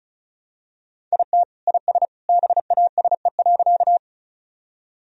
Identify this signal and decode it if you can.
Morse code, heard as a keyed tone, about 34 wpm, 700 Hz. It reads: IT IS BASE.
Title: Sentences from Top 1000 Words - Review Part 1 - 35wpm